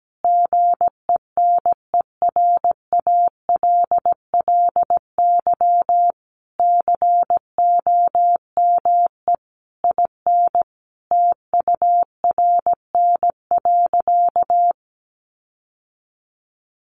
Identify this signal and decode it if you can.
Morse code, heard as a keyed tone, about 17 wpm, 700 Hz. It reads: GENERALLY COME IN TURN.
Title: Sentences from Top 1500 Words 17wpm